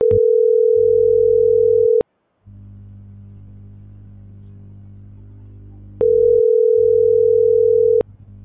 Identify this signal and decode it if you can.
Ringback tone (North America), 440 + 480 Hz.